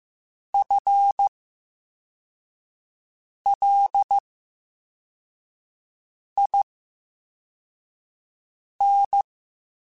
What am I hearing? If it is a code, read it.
Morse code, heard as a keyed tone, about 15 wpm, spaced as for 4 wpm, 800 Hz. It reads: FLIN
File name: Special INGs-mp3s / Special INGs-mp3s / VerySlowToFaster Ing @13